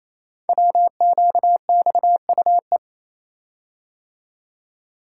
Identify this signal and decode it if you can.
Morse code, heard as a keyed tone, about 28 wpm, 700 Hz. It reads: WQXUE
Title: Code Groups - 5 Letters - 28wpm